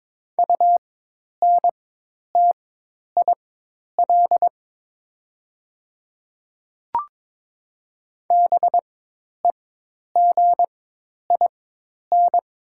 Morse code, heard as a keyed tone, about 22 wpm, spaced as for 10 wpm, 700 Hz. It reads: UNTIL BEGIN
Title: 4x Character Spacing - Top 300 Words - 22wpm